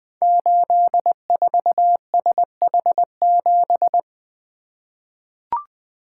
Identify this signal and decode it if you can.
Morse code, heard as a keyed tone, about 20 wpm, 700 Hz. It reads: 84SH7
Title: Code Groups - 5 Letter:Numbers - 20wpm